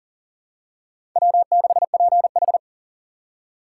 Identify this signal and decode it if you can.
Morse code, heard as a keyed tone, about 40 wpm, 700 Hz. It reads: W6PH
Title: USA Call Signs - 4 Characters - 4x Character Spacing - 40wpm